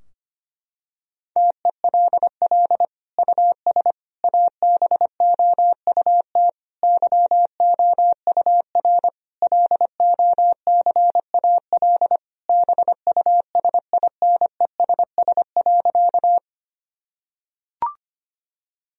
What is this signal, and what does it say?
Morse code, heard as a keyed tone, about 25 wpm, 700 Hz. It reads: TELL US ABOUT YOUR LOCAL BUSINESS.
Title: Sentences from Top 1000 Words - Review Part 3 - 25wpm